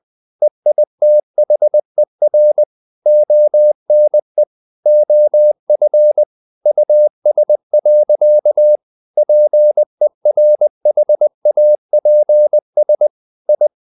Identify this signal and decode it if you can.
Morse code, heard as a keyed tone, about 20 wpm, 600 Hz. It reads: EITHER ONE OF US. PERHAPS I
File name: the old man and the sea / the old man and the sea part 1 20wpm